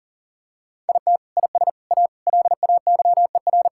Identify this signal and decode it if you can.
Morse code, heard as a keyed tone, about 40 wpm, 700 Hz. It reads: IT IS A LAYER
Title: Sentences from Top 1600 Words 40wpm